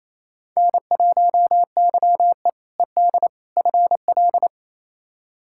Morse code, heard as a keyed tone, about 28 wpm, 700 Hz. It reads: N1YE ED FL
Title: K1USN SST QSO Practice - 28wpm